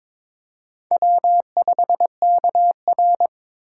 Morse code, heard as a keyed tone, about 26 wpm, 700 Hz. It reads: W5KR
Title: Speed Racing - USA 1x2 Call Signs 22wpm